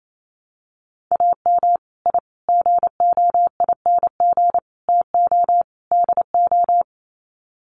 Morse code, heard as a keyed tone, about 28 wpm, 700 Hz. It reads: AM I GOING TO DO
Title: Sets of 5 Words Encore 28wpm